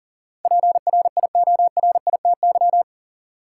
Morse code, heard as a keyed tone, about 40 wpm, 700 Hz. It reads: PRIORITY
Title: Sentences from Top 1600 Words - New Word Drill Alt - 40wpm